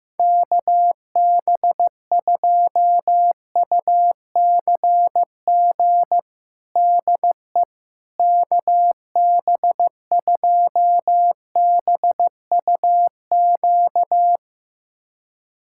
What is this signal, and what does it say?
Morse code, heard as a keyed tone, about 15 wpm, 700 Hz. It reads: KB2UCG DE KB2BUQ
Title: USA Callsign DE Callsign - 6 Characters - 15wpm